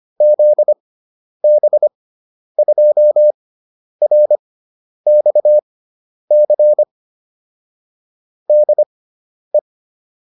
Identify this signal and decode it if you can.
Morse code, heard as a keyed tone, about 25 wpm, spaced as for 10 wpm, 600 Hz. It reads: ZB2RXC DE